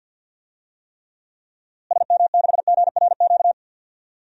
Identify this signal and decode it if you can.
Morse code, heard as a keyed tone, about 53 wpm, 700 Hz. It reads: SM6GRQ
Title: DXCC-100 Call Signs - 6 Characters - Repeated 2x - 50wpm